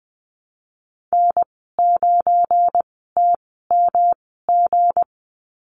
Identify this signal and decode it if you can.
Morse code, heard as a keyed tone, about 20 wpm, spaced as for 15 wpm, 700 Hz. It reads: N9TMG
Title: 173 Lesson 28g - Number 9 - Call Signs 20wpm